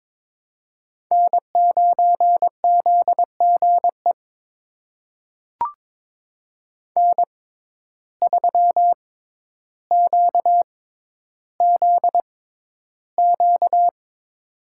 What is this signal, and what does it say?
Morse code, heard as a keyed tone, about 23 wpm, spaced as for 11 wpm, 700 Hz. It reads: N9ZGE N3QZQ
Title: USA Call Signs - 5 Characters - 6x Character Spacing - 22wpm